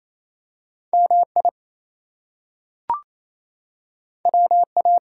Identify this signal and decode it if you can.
Morse code, heard as a keyed tone, about 28 wpm, 700 Hz. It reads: MI WA
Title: US States and Canadian Provinces 28wpm